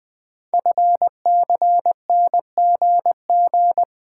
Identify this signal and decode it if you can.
Morse code, heard as a keyed tone, about 20 wpm, 700 Hz. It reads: FCNGG
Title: Code Groups - 5 Letters - 20wpm